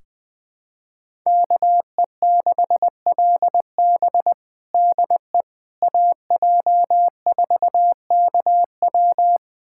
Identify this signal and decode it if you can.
Morse code, heard as a keyed tone, about 21 wpm, 700 Hz. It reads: KE6LB DE AJ4KW
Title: USA Callsign DE Callsign - 5 Characters - 20wpm